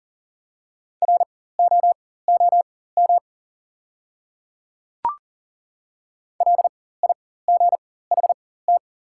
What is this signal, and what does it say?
Morse code, heard as a keyed tone, about 40 wpm, spaced as for 19 wpm, 700 Hz. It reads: ROOM LIGHT
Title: 4x Character Spacing - Top 300 Words - 40wpm